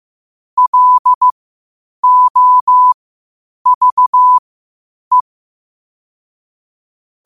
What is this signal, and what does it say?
Morse code, heard as a keyed tone, about 15 wpm, spaced as for 8 wpm, 1000 Hz. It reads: LOVE
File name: qotdfiles / 8wpm